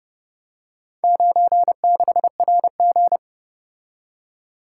Morse code, heard as a keyed tone, about 29 wpm, 700 Hz. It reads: 96RG